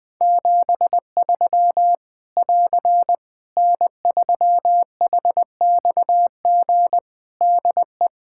Morse code, heard as a keyed tone, about 21 wpm, 700 Hz. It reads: 73 <AR> N3HXG DE
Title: Practice QSO - Exchange 4 - Ending - 20wpm